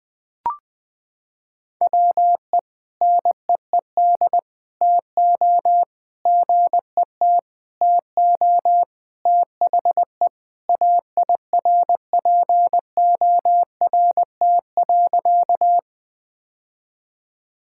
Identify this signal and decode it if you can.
Morse code, heard as a keyed tone, about 20 wpm, 700 Hz. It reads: WE NEED TO GET TO THE AIRPORT.
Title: Sentences with 3-Word Phrases - 20wpm